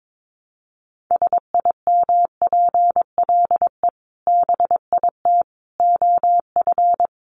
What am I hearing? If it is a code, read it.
Morse code, heard as a keyed tone, about 22 wpm, 700 Hz. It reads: SIMPLE BIT OF
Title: Sentences from Top 900 Words 22wpm